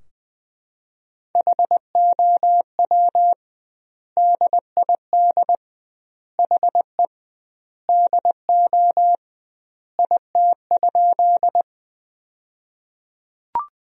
Morse code, heard as a keyed tone, about 21 wpm, 700 Hz. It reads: HOW DID HE DO IT?
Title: 2x Word-Spacing - Sentences from Top 100 Words - 20wpm